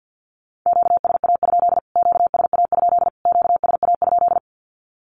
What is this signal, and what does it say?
Morse code, heard as a keyed tone, about 48 wpm, 700 Hz. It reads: QSA? QSA? QSA?
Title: Q-Codes Repeated 3x 50wpm